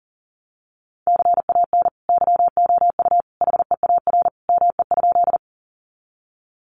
Morse code, heard as a keyed tone, about 40 wpm, 700 Hz. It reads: CAN YOU HEAR ME?
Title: Sentences from Top 300 Words 40wpm